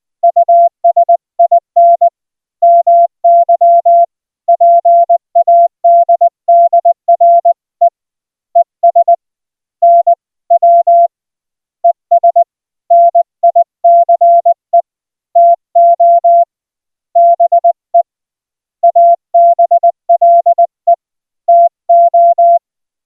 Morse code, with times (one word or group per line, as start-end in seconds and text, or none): 0.2-2.1 USIN
2.6-4.0 MY
4.5-7.9 PADDRE
8.6-9.1 ES
9.8-10.1 N
10.5-11.1 W
11.8-12.4 ES
12.9-14.8 NICE
15.4-16.4 TO
17.2-18.0 BE
18.8-20.9 ABLE
21.5-22.6 TO